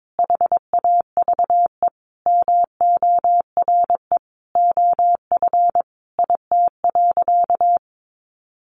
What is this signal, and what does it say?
Morse code, heard as a keyed tone, about 22 wpm, 700 Hz. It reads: HAVE MORE OF IT.
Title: Sentences from Top 1300 Words 22wpm